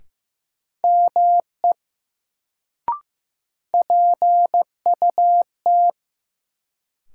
Morse code, heard as a keyed tone, about 15 wpm, 700 Hz. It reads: ME PUT
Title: Sentences from Top 200 Words - New Word Drill - 15wpm